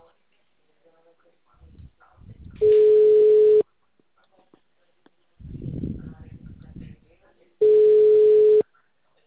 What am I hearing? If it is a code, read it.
Ringback tone (Europe, 425 Hz), with speech.